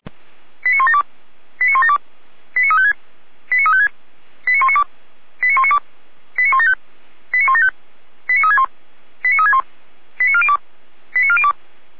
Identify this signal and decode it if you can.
Five-tone selective call (ZVEI), heard as quick runs of five tones, tones of about 80 ms. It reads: 89172 89172 89356 89356 89192 89192 89175 89175 89361 89361 89402 89402